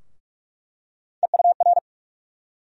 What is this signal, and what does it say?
Morse code, heard as a keyed tone, about 46 wpm, 700 Hz. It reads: EUR
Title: USA Call Sign Postfixes - 3 Letters 45wpm